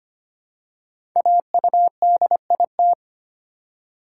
Morse code, heard as a keyed tone, about 25 wpm, 700 Hz. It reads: AUDIT
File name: 5 Letter Words - First 3 Letters Given - 25wpm